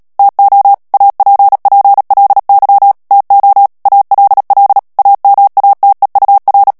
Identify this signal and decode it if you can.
Morse code, heard as a keyed tone, about 37 wpm, 800 Hz. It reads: TO APPLY TO ALL AMATEUR
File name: morse mp3 / ANGLAIS texte clair 3 vitesse 35 mots minute